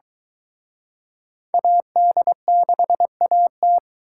Morse code, heard as a keyed tone, about 23 wpm, 700 Hz. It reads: AD6AT